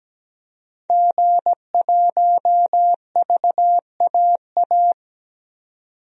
Morse code, heard as a keyed tone, about 17 wpm, 700 Hz. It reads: G1VAA
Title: DXCC-100 Call Signs - 5 Characters - 17wpm